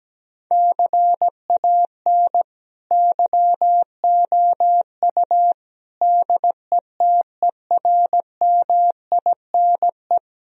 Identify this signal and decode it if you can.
Morse code, heard as a keyed tone, about 17 wpm, 700 Hz. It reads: CAN YOU DETERMINE
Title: Sentences from Top 1000 Words - Review Part 4 - 17wpm